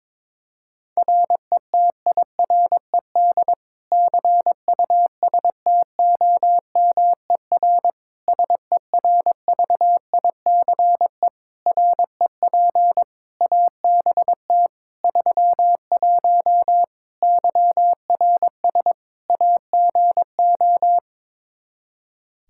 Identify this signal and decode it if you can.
Morse code, heard as a keyed tone, about 22 wpm, 700 Hz. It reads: RETIRED CUSTOMER SERVICE REP ABT 31 YRS AGO
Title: Practice QSO - Exchange 3 - More Info - 22wpm